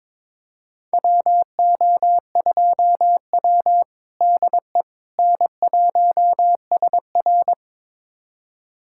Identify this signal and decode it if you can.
Morse code, heard as a keyed tone, about 22 wpm, 700 Hz. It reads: WO2W DE N1SR
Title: USA Callsign DE Callsign - 4 Characters - 22wpm